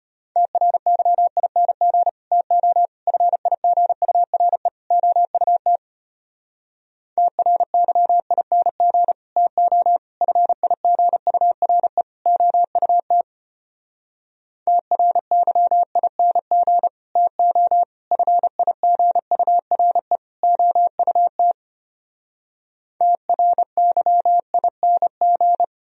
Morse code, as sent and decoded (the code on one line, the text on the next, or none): - .-. -.-- .. -. --. / - --- / ..-. .. --. ..- .-. / --- ..- - / - .-. -.-- .. -. --. / - --- / ..-. .. --. ..- .-. / --- ..- - / - .-. -.-- .. -. --. / - --- / ..-. .. --. ..- .-. / --- ..- - / - .-. -.-- .. -. --.
TRYING TO FIGURE OUT TRYING TO FIGURE OUT TRYING TO FIGURE OUT TRYING